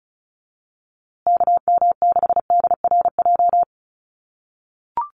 Morse code, heard as a keyed tone, about 34 wpm, 700 Hz. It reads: KM6DRJ
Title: USA Call Signs Repeated 3x 35wpm